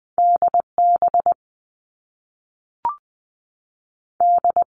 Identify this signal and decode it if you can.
Morse code, heard as a keyed tone, about 20 wpm, 700 Hz. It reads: D B D
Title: All Cut Numbers - 2-Digit 20wpm